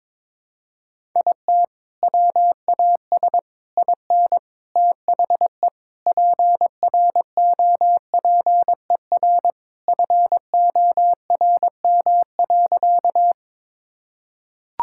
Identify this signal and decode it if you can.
Morse code, heard as a keyed tone, about 22 wpm, 700 Hz. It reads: IT WAS IN THE PROPER FORM.
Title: Sentences from Top 1000 Words - Review Part 5 - 22wpm